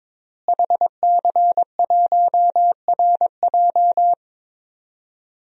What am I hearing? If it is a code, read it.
Morse code, heard as a keyed tone, about 21 wpm, 700 Hz. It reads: HC1RJ
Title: DXCC-100 Call Signs - 5 Characters - Repeated 3x - 22wpm